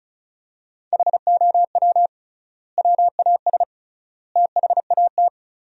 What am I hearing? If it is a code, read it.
Morse code, heard as a keyed tone, about 34 wpm, 700 Hz. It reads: HOW WAS THAT
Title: 3x Word-Spacing - Sentences from Top 500 Words - 35wpm